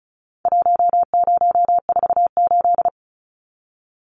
Morse code, heard as a keyed tone, about 34 wpm, 700 Hz. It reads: 1048
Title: Numbers - Contest Style - 4-Digit 35wpm